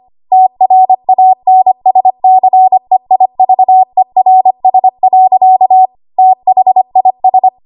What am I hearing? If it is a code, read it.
Morse code, heard as a keyed tone, about 25 wpm, 750 Hz. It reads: TRANSCEIVERS. THIS
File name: Morse / 25WPM